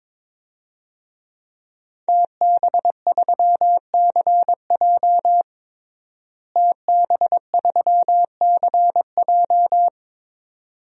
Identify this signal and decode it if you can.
Morse code, heard as a keyed tone, about 21 wpm, 700 Hz. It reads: TB3CJ TB3CJ
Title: DXCC-100 Call Signs - 5 Characters - Repeated 2x - 22wpm